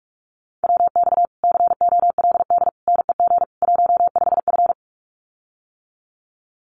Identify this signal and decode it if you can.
Morse code, heard as a keyed tone, about 44 wpm, 700 Hz. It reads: WX COLD NEG 15F
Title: Common QSO Phrases Encore 45wpm